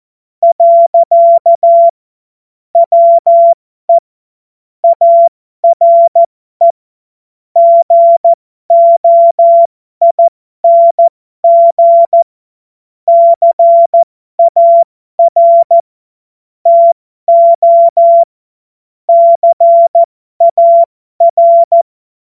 Morse code, heard as a keyed tone, about 14 wpm, spaced as for 12 wpm, 680 Hz. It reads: . WE ARE GOING CAR TO CAR
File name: Morsekurs / Witze